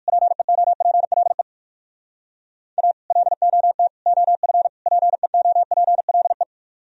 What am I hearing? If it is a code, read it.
Morse code, heard as a keyed tone, about 60 wpm, 700 Hz. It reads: PEOPLE A LOT OF PEOPLE